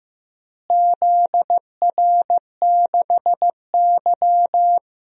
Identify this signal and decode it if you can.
Morse code, heard as a keyed tone, about 15 wpm, 700 Hz. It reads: ZR6Y
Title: DXCC-100 Call Signs - 4 Characters - Repeated 2x - 15wpm